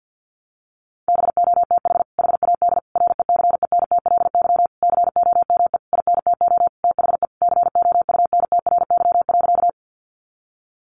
Morse code, heard as a keyed tone, about 50 wpm, 700 Hz. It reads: BOTH HAD RECENTLY COME INTO THE COUNTRY.
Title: Sentences from Top 1200 Words 50wpm